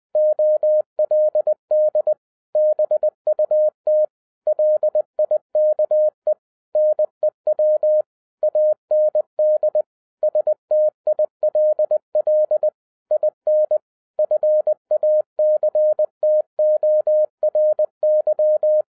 Morse code, heard as a keyed tone, about 20 wpm, 610 Hz. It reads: OLD BUT LIKE NEW AND STILL IN FACTORY
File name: 2011-07-mp3 / n43 Czech Army Morse Keys 20wpm